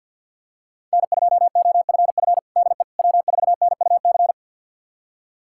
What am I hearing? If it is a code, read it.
Morse code, heard as a keyed tone, about 52 wpm, 700 Hz. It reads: N2OFF DE W4NUC
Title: USA Callsign DE Callsign - 5 Characters - 50wpm